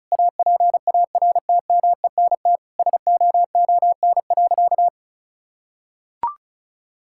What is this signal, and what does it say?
Morse code, heard as a keyed tone, about 34 wpm, 700 Hz. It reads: APARTMENT SOON.
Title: Sentences from Top 1000 Words - Review Part 6 - 35wpm